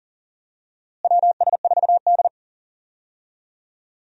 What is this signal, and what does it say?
Morse code, heard as a keyed tone, about 39 wpm, 700 Hz. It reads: WS4D
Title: USA Call Signs - 4 Characters - Repeated 2x - 40wpm